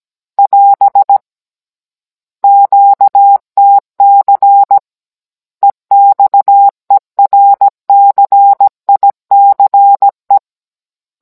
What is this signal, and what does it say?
Morse code, heard as a keyed tone, about 17 wpm, 800 Hz. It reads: <AS> QTC EXERCICE
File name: Cours-NR18-16mm